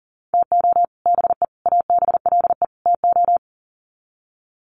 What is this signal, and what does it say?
Morse code, heard as a keyed tone, about 40 wpm, 700 Hz. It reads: TO BE ABLE TO
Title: Sets of 4 Words 40wpm